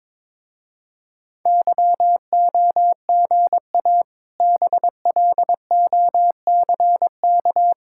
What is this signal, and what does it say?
Morse code, heard as a keyed tone, about 22 wpm, 700 Hz. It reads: YOGA BLOCK
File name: Difficult Words with Hints - 22wpm